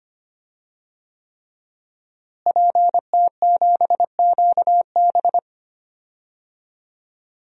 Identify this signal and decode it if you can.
Morse code, heard as a keyed tone, about 25 wpm, 700 Hz. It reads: PT7QB
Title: DXCC-100 Call Signs - 5 Characters - Repeated 3x - 25wpm